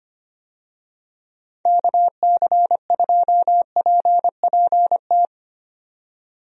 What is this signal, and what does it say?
Morse code, heard as a keyed tone, about 25 wpm, 700 Hz. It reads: KC2PPT
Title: USA Call Signs - 6 Characters - 25wpm